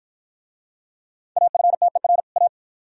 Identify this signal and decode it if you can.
Morse code, heard as a keyed tone, about 53 wpm, 700 Hz. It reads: AFTER A